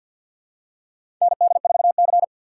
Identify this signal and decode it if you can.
Morse code, heard as a keyed tone, about 52 wpm, 700 Hz. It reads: ND4C
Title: USA Call Signs - 4 Characters - 3x Character Spacing - 50wpm